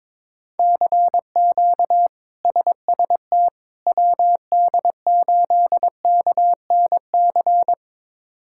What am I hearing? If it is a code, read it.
Morse code, heard as a keyed tone, about 22 wpm, 700 Hz. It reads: CQ SST WD8KNC
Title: K1USN SST QSO Practice - 22wpm